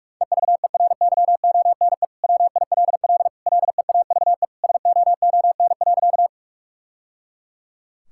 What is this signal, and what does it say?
Morse code, heard as a keyed tone, about 46 wpm, 700 Hz. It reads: EVERYONE WILL LEAVE SOON.
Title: Sentences from Top 1000 Words - Review Part 5 - 45wpm